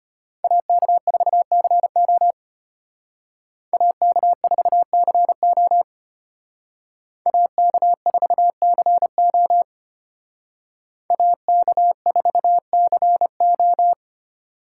Morse code, heard as a keyed tone, about 30 wpm, 700 Hz. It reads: AK4CO AK4CO AK4CO AK4CO
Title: Speed Racing - USA 2x2 Call Signs 25wpm